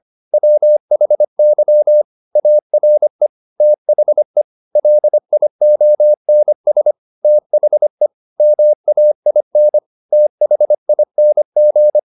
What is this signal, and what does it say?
Morse code, heard as a keyed tone, about 25 wpm, 600 Hz. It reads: WHY ARE THE LIONS THE MAIN THING
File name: the old man and the sea part 1 25wpm